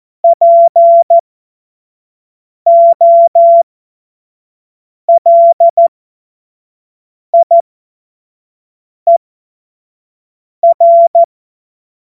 Morse code, heard as a keyed tone, about 14 wpm, spaced as for 5 wpm, 680 Hz. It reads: POLIER
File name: Morsekurs / Witze / Witz 30. Dezember 2025 25 F70